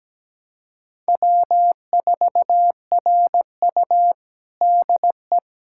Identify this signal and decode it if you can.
Morse code, heard as a keyed tone, about 17 wpm, 700 Hz. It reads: W4RU DE
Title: USA Callsign DE Callsign - 4 Characters - 17wpm